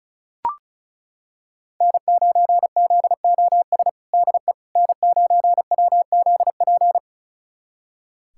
Morse code, heard as a keyed tone, about 36 wpm, 700 Hz. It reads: N9ZOS DE N9WZP
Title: USA Callsign DE Callsign - 5 Characters - 35wpm